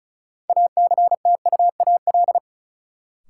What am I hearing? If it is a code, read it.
Morse code, heard as a keyed tone, about 36 wpm, 700 Hz. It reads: ACTUAL
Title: Sentences from Top 1000 Words - Review Part 5 - New Word Drill - 35wpm